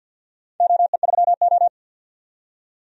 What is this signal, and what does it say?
Morse code, heard as a keyed tone, about 50 wpm, 700 Hz. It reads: OE3O
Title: DXCC-100 Call Signs - 4 Characters - 50wpm